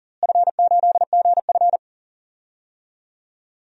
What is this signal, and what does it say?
Morse code, heard as a keyed tone, about 40 wpm, 700 Hz. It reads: F8GF
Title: DXCC-100 Call Signs - 4 Characters - Repeated 3x - 40wpm